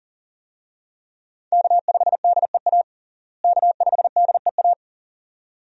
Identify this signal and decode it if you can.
Morse code, heard as a keyed tone, about 40 wpm, 700 Hz. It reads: K5DEA K5DEA
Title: USA Call Signs - 5 Characters - Repeated 2x - 40wpm